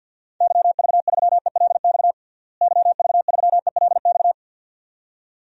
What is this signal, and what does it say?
Morse code, heard as a keyed tone, about 49 wpm, 700 Hz. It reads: YV3ELX YV3ELX
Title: DXCC-100 Call Signs - 6 Characters - Repeated 2x - 50wpm